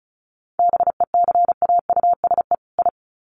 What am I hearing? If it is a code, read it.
Morse code, heard as a keyed tone, about 34 wpm, 700 Hz. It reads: BECAUSE I
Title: Sets of 2 Words 35wpm